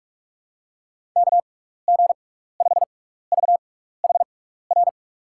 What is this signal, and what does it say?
Morse code, heard as a keyed tone, about 46 wpm, spaced as for 15 wpm, 700 Hz. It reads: KG5VHR
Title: USA Call Signs - 6x Character Spacing - 45wpm